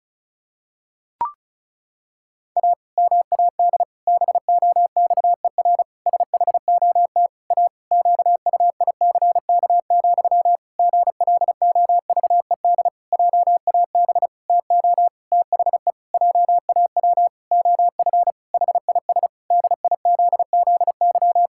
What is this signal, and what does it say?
Morse code, heard as a keyed tone, about 34 wpm, 700 Hz. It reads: A MAD BOXER SHOT A QUICK, GLOVED JAB TO THE JAW OF HIS DIZZY
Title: Pangram Challenge v2 - 35wpm